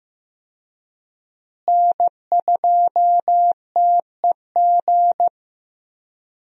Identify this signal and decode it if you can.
Morse code, heard as a keyed tone, about 15 wpm, 700 Hz. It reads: N2TEG
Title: USA Call Signs - 5 Characters - Repeated 2x - 15wpm